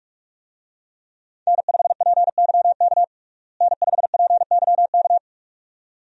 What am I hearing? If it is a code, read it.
Morse code, heard as a keyed tone, about 42 wpm, 700 Hz. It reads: N5PYK N5PYK
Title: USA Call Signs - 5 Characters - Repeated 2x - 45wpm